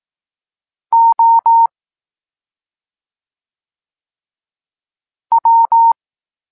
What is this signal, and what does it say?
Morse code, heard as a keyed tone, about 18 wpm, 920 Hz. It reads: O W
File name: Tekst-26 1